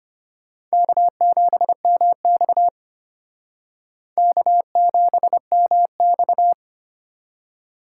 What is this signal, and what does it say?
Morse code, heard as a keyed tone, about 32 wpm, 700 Hz. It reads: K7MX K7MX